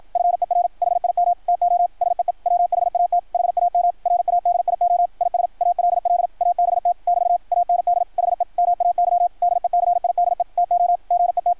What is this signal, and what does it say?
Morse code, heard as a keyed tone, about 53 wpm, 660 Hz, with radio noise. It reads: FER HIM TO SEE WHAT HAM RADIO IS ALL ABT <BT> AND HE MAY DECIDE TO GET
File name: archived / CRSnet